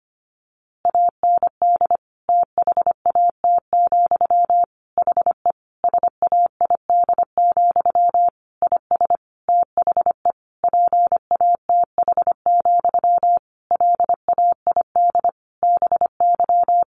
Morse code, heard as a keyed tone, about 25 wpm, 700 Hz. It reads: AND THAT, HE SAID, IS THE PATH, LAID BY
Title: A Sound Of Thunder 25wpm